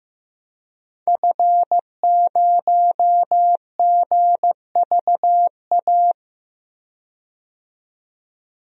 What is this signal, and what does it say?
Morse code, heard as a keyed tone, about 15 wpm, 700 Hz. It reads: F0GVA
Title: DXCC-100 Call Signs - 5 Characters - 15wpm